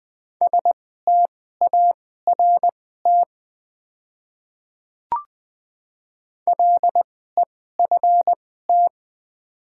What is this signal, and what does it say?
Morse code, heard as a keyed tone, about 20 wpm, spaced as for 14 wpm, 700 Hz. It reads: START LEFT
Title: 2x Character Spacing - Top 300 Words - 20wpm